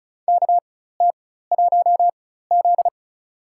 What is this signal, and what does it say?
Morse code, heard as a keyed tone, about 36 wpm, spaced as for 16 wpm, 700 Hz. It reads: KT1Z
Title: USA Call Signs - 4 Characters - 4x Character Spacing - 35wpm